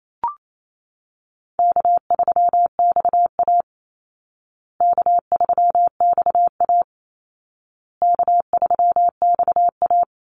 Morse code, heard as a keyed tone, about 28 wpm, 700 Hz. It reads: K3XA K3XA K3XA
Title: Easy Call Signs Repeated 3x - 28wpm